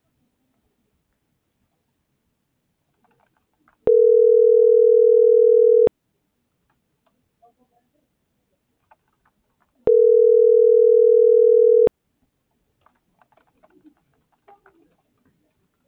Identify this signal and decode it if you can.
Ringback tone (North America), 440 + 480 Hz.
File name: CENTRAL TERRESTRE / out karis - .78079